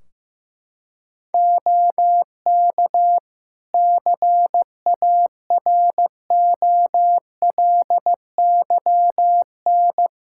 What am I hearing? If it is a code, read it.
Morse code, heard as a keyed tone, about 16 wpm, 700 Hz. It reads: OK CAROLYN
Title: Practice QSO - Exchange 4 - Ending - 15wpm